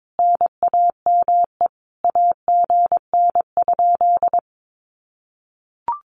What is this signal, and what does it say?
Morse code, heard as a keyed tone, about 22 wpm, 700 Hz. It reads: NAME AGN?
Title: Common QSO Phrases Encore 22wpm